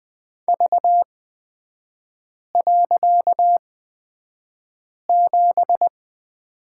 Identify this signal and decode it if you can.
Morse code, heard as a keyed tone, about 20 wpm, spaced as for 5 wpm, 700 Hz. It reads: V.7
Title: 255 Lesson 41e - Period - Three Characters 20wpm